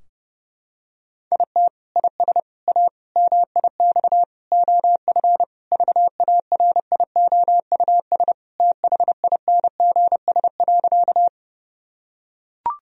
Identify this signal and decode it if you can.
Morse code, heard as a keyed tone, about 30 wpm, 700 Hz. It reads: IT IS A MIX OF VARIOUS THINGS.
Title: Sentences from Top 1000 Words - Review Part 4 - 30wpm